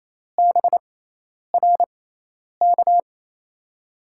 Morse code, heard as a keyed tone, about 28 wpm, spaced as for 10 wpm, 700 Hz. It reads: BRK
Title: USA Call Signs - 6 Characters - 6x Character Spacing - 28wpm